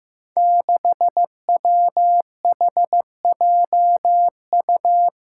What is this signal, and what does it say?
Morse code, heard as a keyed tone, about 15 wpm, 700 Hz. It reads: 6WHJU
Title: Five Challenging Letter-Number Combinations 15wpm